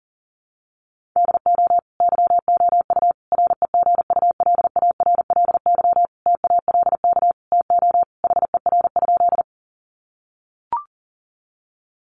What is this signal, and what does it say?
Morse code, heard as a keyed tone, about 40 wpm, 700 Hz. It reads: DO YOU REGULARLY TALK TO HER?
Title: Sentences from Top 1400 Words 40wpm